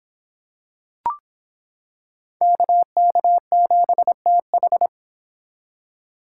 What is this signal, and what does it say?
Morse code, heard as a keyed tone, about 25 wpm, 700 Hz. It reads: KK7TH